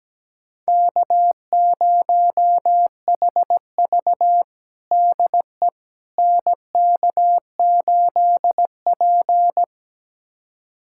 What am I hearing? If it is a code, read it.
Morse code, heard as a keyed tone, about 17 wpm, 700 Hz. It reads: K0HV DE NK8P
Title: USA Callsign DE Callsign - 4 Characters - 17wpm